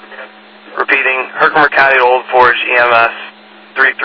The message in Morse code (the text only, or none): none